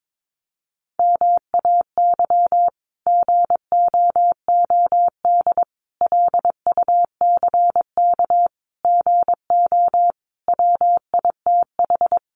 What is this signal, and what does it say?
Morse code, heard as a keyed tone, about 22 wpm, 700 Hz. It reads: MAY GOOD LUCK GO WITH
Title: Sentences from Top 1500 Words 22wpm